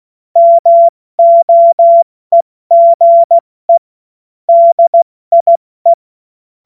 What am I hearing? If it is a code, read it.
Morse code, heard as a keyed tone, about 16 wpm, 680 Hz. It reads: MOEGE DIE